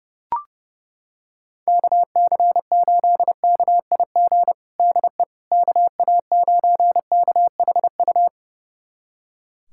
Morse code, heard as a keyed tone, about 30 wpm, 700 Hz. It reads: KC8KIG DE KA9KHU
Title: USA Callsign DE Callsign - 6 Characters - 30wpm